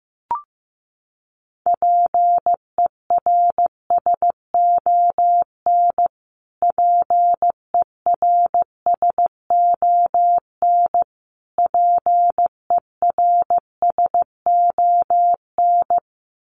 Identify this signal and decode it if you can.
Morse code, heard as a keyed tone, about 15 wpm, 700 Hz. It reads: PERSON PERSON PERSON
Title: Top 500 Words Repeated 15wpm